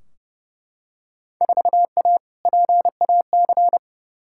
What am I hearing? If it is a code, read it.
Morse code, heard as a keyed tone, about 30 wpm, 700 Hz. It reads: 4A PAC
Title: Field Day Exchanges Encore - 30wpm